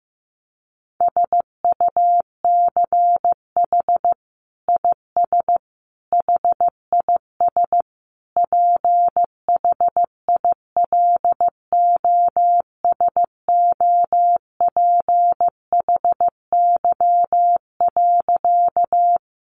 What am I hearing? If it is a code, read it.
Morse code, heard as a keyed tone, about 15 wpm, 700 Hz. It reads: SUCH IS HIS PHILOSOPHY.
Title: Sentences from Top 900 Words 15wpm